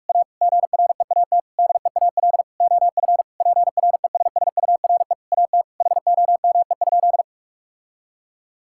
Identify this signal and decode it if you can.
Morse code, heard as a keyed tone, about 46 wpm, 700 Hz. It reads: A GREAT DEAL OF PRESSURE AT HOME?
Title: Sentences from Top 1000 Words - Review Part 6 - 45wpm